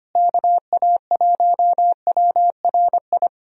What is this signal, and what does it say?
Morse code, heard as a keyed tone, about 25 wpm, 700 Hz. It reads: KA1WRI
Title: USA Call Signs - 6 Characters - Repeated 2x - 25wpm